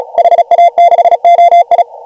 Morse code, heard as a keyed tone, about 36 wpm, 720 Hz. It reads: HA6OI
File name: AUDIO2019 / HG1G CW